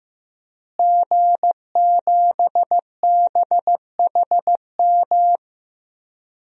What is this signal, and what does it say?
Morse code, heard as a keyed tone, about 15 wpm, 700 Hz. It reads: G7BHM
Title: DXCC-100 Call Signs - 5 Characters - Repeated 3x - 15wpm